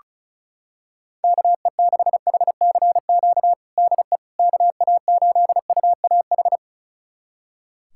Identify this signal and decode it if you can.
Morse code, heard as a keyed tone, about 36 wpm, 700 Hz. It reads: KE6HCQ DE KA8UAH
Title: USA Callsign DE Callsign 35wpm